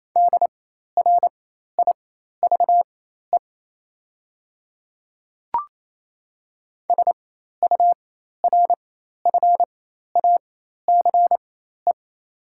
Morse code, heard as a keyed tone, about 28 wpm, spaced as for 13 wpm, 700 Hz. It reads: DRIVE SURFACE
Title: 4x Character Spacing - Top 500 Words - 28wpm